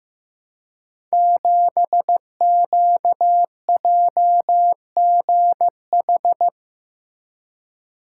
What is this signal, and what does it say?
Morse code, heard as a keyed tone, about 15 wpm, 700 Hz. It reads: 7QJGH
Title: Code Groups - 5 Letter:Numbers - 15wpm